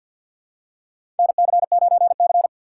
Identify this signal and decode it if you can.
Morse code, heard as a keyed tone, about 50 wpm, 700 Hz. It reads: NC9C
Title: USA Call Signs - 4 Characters - 3x Character Spacing - 50wpm